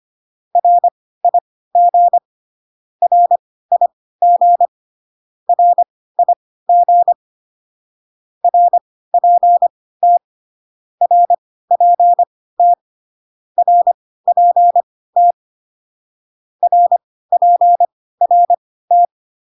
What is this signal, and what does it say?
Morse code, heard as a keyed tone, about 25 wpm, spaced as for 16 wpm, 700 Hz. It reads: RIG RIG RIG RPT RPT RPT RPRT